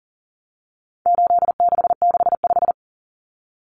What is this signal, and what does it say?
Morse code, heard as a keyed tone, about 40 wpm, 700 Hz. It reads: 8665
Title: Numbers - 4-Digit 40wpm